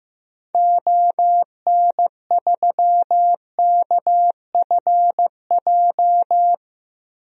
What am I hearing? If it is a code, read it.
Morse code, heard as a keyed tone, about 15 wpm, 700 Hz. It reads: ON3KFJ
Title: DXCC-100 Call Signs - 6 Characters - Repeated 3x - 15wpm